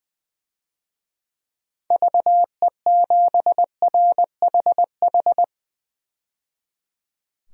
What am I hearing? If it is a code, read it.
Morse code, heard as a keyed tone, about 20 wpm, 700 Hz. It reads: VE7RHH
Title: DXCC-100 Call Signs - 6 Characters - 20wpm